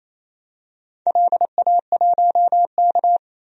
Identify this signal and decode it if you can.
Morse code, heard as a keyed tone, about 28 wpm, 700 Hz. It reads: LA1K
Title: DXCC-100 Call Signs - 4 Characters - 28wpm